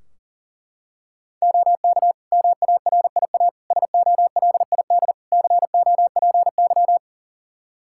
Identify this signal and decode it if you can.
Morse code, heard as a keyed tone, about 37 wpm, 700 Hz. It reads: OK MARIA SOLID COPY
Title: Practice QSO - Exchange 3 - More Info - 40wpm